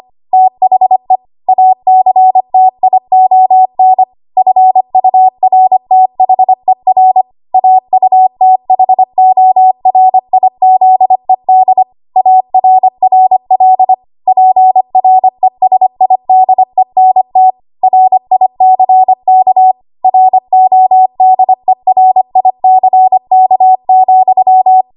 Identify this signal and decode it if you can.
Morse code, heard as a keyed tone, about 25 wpm, 750 Hz. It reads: THE ACTION FURTHER AUTHORIZED ARRL PRESIDENT RICK RODERICK,
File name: Archive / 25 WPM / 25WPM